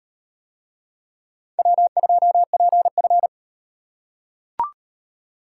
Morse code, heard as a keyed tone, about 38 wpm, 700 Hz. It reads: W2PF